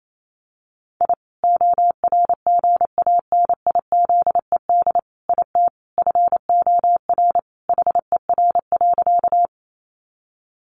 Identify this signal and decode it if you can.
Morse code, heard as a keyed tone, about 28 wpm, 700 Hz. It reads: I ORGANIZED IT FOR HER.
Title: Sentences from Top 1300 Words 28wpm